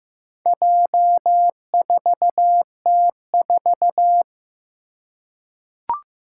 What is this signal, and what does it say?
Morse code, heard as a keyed tone, about 15 wpm, 700 Hz. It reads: J4T4
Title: Four Challenging Letter-Number Combinations 15wpm